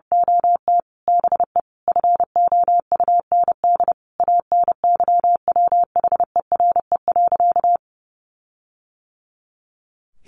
Morse code, heard as a keyed tone, about 30 wpm, 700 Hz. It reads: OT BE FOUND ANYWHERE.
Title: Sentences from Top 1400 Words 30wpm